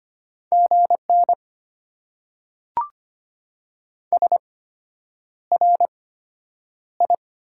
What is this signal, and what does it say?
Morse code, heard as a keyed tone, about 25 wpm, spaced as for 9 wpm, 690 Hz. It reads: GN SRI
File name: Top 90 QSO Elements - 8x Character Spacing - 25wpm